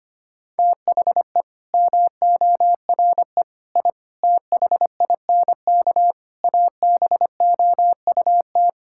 Morse code, heard as a keyed tone, about 25 wpm, 700 Hz. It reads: THE MORE I THINK ABOUT
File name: Sets of 9 Words - 25wpm